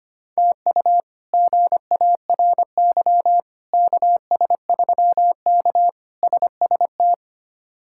Morse code, heard as a keyed tone, about 25 wpm, 700 Hz. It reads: TU GARY KS3K SST
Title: K1USN SST QSO Practice - 25wpm